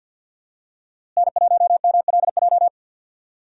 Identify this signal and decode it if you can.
Morse code, heard as a keyed tone, about 50 wpm, 700 Hz. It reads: N1MLJ
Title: USA Call Signs - 5 Characters - Repeated 3x - 50wpm